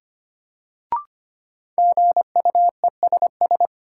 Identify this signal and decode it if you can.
Morse code, heard as a keyed tone, about 25 wpm, 700 Hz. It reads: GUESS
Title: Sentences from Top 1500 Words - New Word Drill - 25wpm